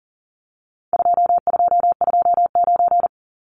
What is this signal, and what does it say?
Morse code, heard as a keyed tone, about 40 wpm, 700 Hz. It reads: 2229
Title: Numbers - 4-Digit 40wpm